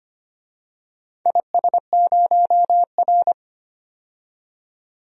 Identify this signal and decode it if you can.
Morse code, heard as a keyed tone, about 25 wpm, 690 Hz. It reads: IS0R
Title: DXCC-100 Call Signs - 4 Characters - 25wpm